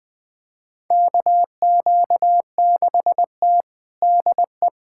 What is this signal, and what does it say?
Morse code, heard as a keyed tone, about 21 wpm, 700 Hz. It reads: KQ6T DE
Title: USA Callsign DE Callsign - 4 Characters - 20wpm